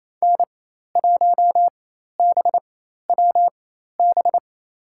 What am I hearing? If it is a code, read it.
Morse code, heard as a keyed tone, about 28 wpm, spaced as for 13 wpm, 700 Hz. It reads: N1BWB
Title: USA Call Signs - 5 Characters - 4x Character Spacing - 28wpm